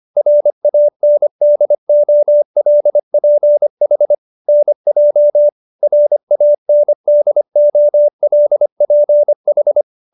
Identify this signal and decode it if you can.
Morse code, heard as a keyed tone, about 25 wpm, 590 Hz. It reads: RANDOLPH NJ RANDOLPH